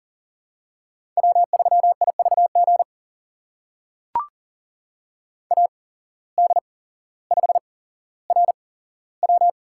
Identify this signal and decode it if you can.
Morse code, heard as a keyed tone, about 40 wpm, spaced as for 17 wpm, 700 Hz. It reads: W3IVG AD5RW
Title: USA Call Signs - 5 Characters - 8x Character Spacing - 40wpm